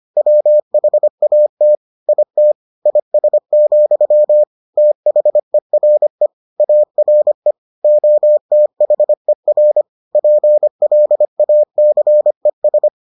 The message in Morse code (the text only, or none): WHAT IT IS, THERE ARE OTHER PLACES